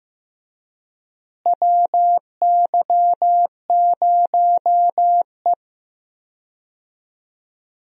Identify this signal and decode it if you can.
Morse code, heard as a keyed tone, about 16 wpm, 700 Hz. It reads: WY0E
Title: USA Call Signs - 4 Characters - Repeated 3x - 15wpm